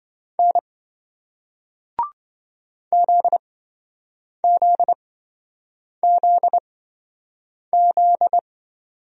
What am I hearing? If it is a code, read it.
Morse code, heard as a keyed tone, about 27 wpm, spaced as for 7 wpm, 700 Hz. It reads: N ZZZZ